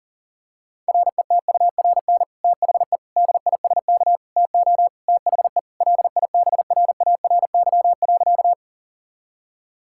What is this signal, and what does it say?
Morse code, heard as a keyed tone, about 40 wpm, 700 Hz. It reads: RETURN THE DISK TO THE LIBRARY.
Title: Sentences from Top 1000 Words - Review Part 6 - 40wpm